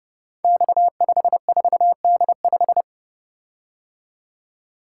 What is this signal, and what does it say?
Morse code, heard as a keyed tone, about 29 wpm, 700 Hz. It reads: X54D5
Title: Code Groups - 5 Letter:Numbers - 30wpm